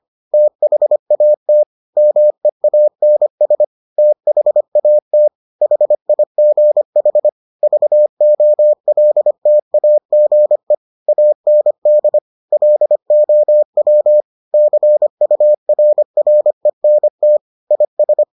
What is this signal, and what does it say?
Morse code, heard as a keyed tone, about 25 wpm, 600 Hz. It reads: THAT MEANS THAT HIGH VOLTAGE AND LOW CURRENT IS